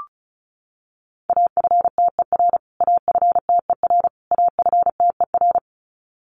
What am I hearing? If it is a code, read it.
Morse code, heard as a keyed tone, about 34 wpm, 700 Hz. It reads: AFTER AFTER AFTER
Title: Top 500 Words Repeated 35wpm